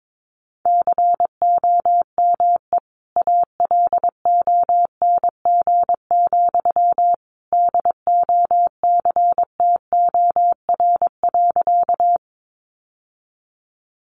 Morse code, heard as a keyed tone, about 22 wpm, 700 Hz. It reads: COME ALONG, DOCTOR.
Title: Sentences from Top 1500 Words 22wpm